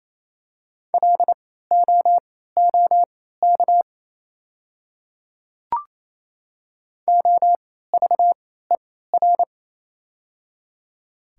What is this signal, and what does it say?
Morse code, heard as a keyed tone, about 28 wpm, spaced as for 16 wpm, 700 Hz. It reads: LOOK OVER
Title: 3x Character Spacing - Top 100 Words - 28wpm